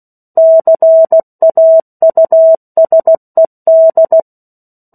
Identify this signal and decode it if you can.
Morse code, heard as a keyed tone, about 16 wpm, 650 Hz. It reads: CAUSED
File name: 03-Chapter-03 16 WPM